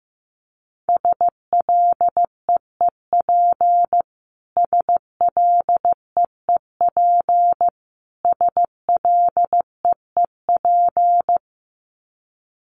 Morse code, heard as a keyed tone, about 15 wpm, 700 Hz. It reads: SLEEP SLEEP SLEEP
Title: Top 500 Words Repeated 15wpm